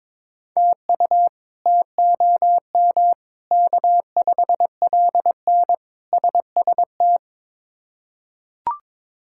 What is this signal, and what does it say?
Morse code, heard as a keyed tone, about 22 wpm, 700 Hz. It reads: TU TOM K5LN SST
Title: K1USN SST QSO Practice - 22wpm